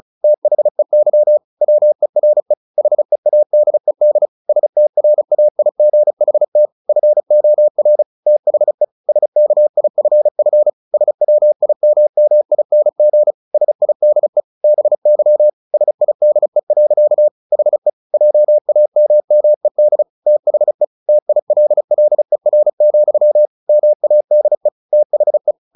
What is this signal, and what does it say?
Morse code, heard as a keyed tone, about 34 wpm, 600 Hz. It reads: THEY WERE HEADED STRAIGHT FOR THE SKIFF SWIMMING SIDE BY SIDE. HE JAMMED THE TILLER, MADE THE
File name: the old man and the sea 2 35wpm